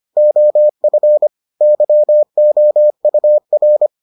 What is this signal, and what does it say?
Morse code, heard as a keyed tone, about 25 wpm, 600 Hz. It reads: OF YOUR